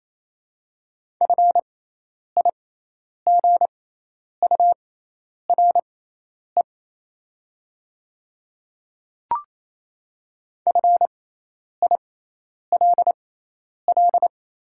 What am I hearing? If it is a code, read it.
Morse code, heard as a keyed tone, about 28 wpm, spaced as for 10 wpm, 700 Hz. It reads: FIGURE FILL
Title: 6x Character Spacing - Top 500 Words - 28wpm